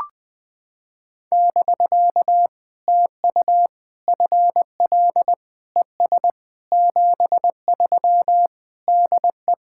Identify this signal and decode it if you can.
Morse code, heard as a keyed tone, about 20 wpm, 700 Hz. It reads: <BK> TU FL ES 73 DE
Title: POTA QSO Practice - 20wpm